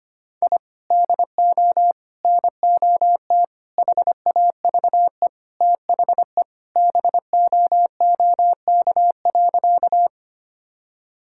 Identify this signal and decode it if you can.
Morse code, heard as a keyed tone, about 25 wpm, 700 Hz. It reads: I DO NOT HAVE THE BOOK.
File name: practice / 25wpm / Sentences with 4-Word Phrases - 25wpm